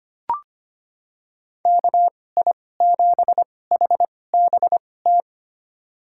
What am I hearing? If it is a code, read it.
Morse code, heard as a keyed tone, about 25 wpm, spaced as for 18 wpm, 700 Hz. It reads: KI7HBT
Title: USA Call Signs - 6 Characters - 2x Character Spacing - 25wpm